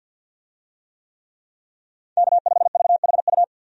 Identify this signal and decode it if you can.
Morse code, heard as a keyed tone, about 51 wpm, 700 Hz. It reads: K5VHU
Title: USA Call Signs - 5 Characters - 50wpm